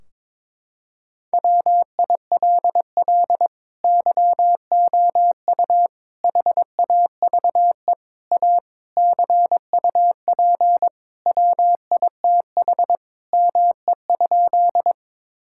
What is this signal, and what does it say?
Morse code, heard as a keyed tone, about 22 wpm, 700 Hz. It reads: WILL YOU HAVE A CUP WITH ME?
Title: Sentences from Top 1600 Words 22wpm